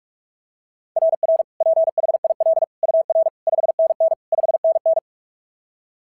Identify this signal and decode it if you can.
Morse code, heard as a keyed tone, about 43 wpm, 690 Hz. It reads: RR PHIL UR 5NN 5NN
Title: SOTA QSO Practice 45wpm